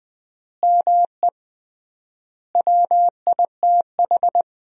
Morse code, heard as a keyed tone, about 20 wpm, 700 Hz. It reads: ME WITH
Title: 3x Word-Spacing - Sentences from Top 500 Words - 20wpm